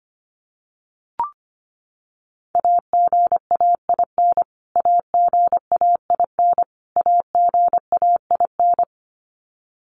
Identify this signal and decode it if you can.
Morse code, heard as a keyed tone, about 25 wpm, 700 Hz. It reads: AGAIN AGAIN AGAIN
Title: Top 200 Words Repeated 25wpm